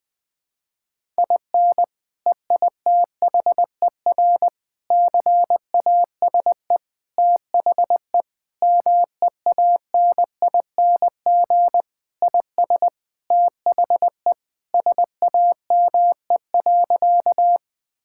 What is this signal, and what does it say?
Morse code, heard as a keyed tone, about 21 wpm, 700 Hz. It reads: IN EITHER CASE THE MEANING IS THE SAME.
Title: Sentences from Top 1000 Words - Review Part 6 - 20wpm